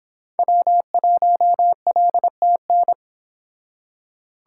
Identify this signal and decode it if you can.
Morse code, heard as a keyed tone, about 26 wpm, 700 Hz. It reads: W1LTN